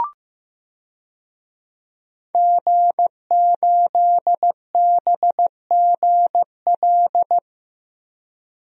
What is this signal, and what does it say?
Morse code, heard as a keyed tone, about 15 wpm, 700 Hz. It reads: G8BGL
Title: DXCC-100 Call Signs - 5 Characters - Repeated 3x - 15wpm